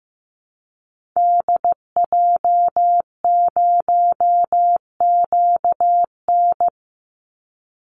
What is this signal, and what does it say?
Morse code, heard as a keyed tone, about 15 wpm, 700 Hz. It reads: DJ0QN
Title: Call Signs Repeated 3x - 15wpm